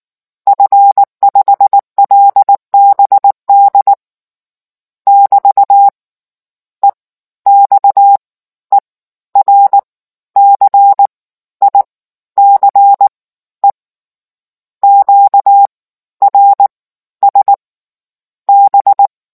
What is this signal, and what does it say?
Morse code, heard as a keyed tone, about 19 wpm, spaced as for 12 wpm, 800 Hz. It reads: F5LBD <BT> EXERCICE QRS B